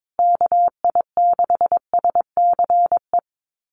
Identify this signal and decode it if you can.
Morse code, heard as a keyed tone, about 22 wpm, 700 Hz. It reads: KI6SCE
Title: USA Call Signs Repeated 2x 22wpm